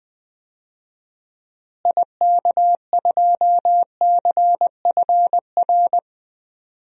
Morse code, heard as a keyed tone, about 20 wpm, 700 Hz. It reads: IK2CFR
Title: DXCC-100 Call Signs - 6 Characters - Repeated 2x - 20wpm